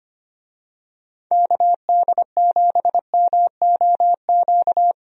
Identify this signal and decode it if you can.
Morse code, heard as a keyed tone, about 25 wpm, 700 Hz. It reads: KD7MOQ
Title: USA Call Signs - 6 Characters - Repeated 3x - 25wpm